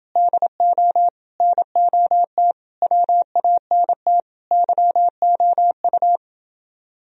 Morse code, heard as a keyed tone, about 28 wpm, 700 Hz. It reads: DO NOT WANT YOU